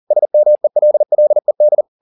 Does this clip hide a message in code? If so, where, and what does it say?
Yes: Morse code from 0.1 to 1.8 s, heard as a keyed tone, about 40 wpm, 600 Hz. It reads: SMELLED